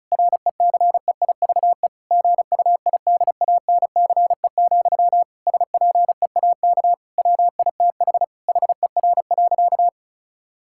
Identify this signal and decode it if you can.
Morse code, heard as a keyed tone, about 34 wpm, 700 Hz. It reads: RECEIVE GUIDANCE, SPEAK WITH HER.
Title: Sentences from Top 1600 Words 35wpm